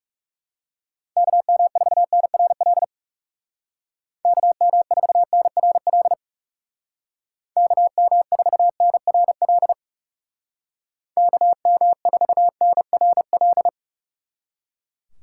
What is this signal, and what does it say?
Morse code, heard as a keyed tone, about 35 wpm, 700 Hz. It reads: KM4NRL KM4NRL KM4NRL KM4NRL
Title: Speed Racing - USA 2x3 Call Signs 30wpm